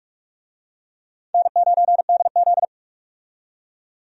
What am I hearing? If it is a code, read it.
Morse code, heard as a keyed tone, about 46 wpm, 700 Hz. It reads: N9DZ
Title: USA Call Signs - 4 Characters - 45wpm